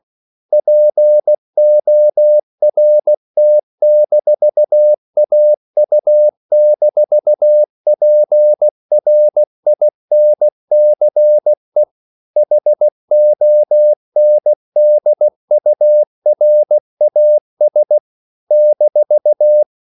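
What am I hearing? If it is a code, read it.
Morse code, heard as a keyed tone, about 16 wpm, 600 Hz. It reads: PORT-AU-PRINCE HONDURAS -
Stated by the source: CW practice of national capitals at 16 words per minute, from ON6ZQ.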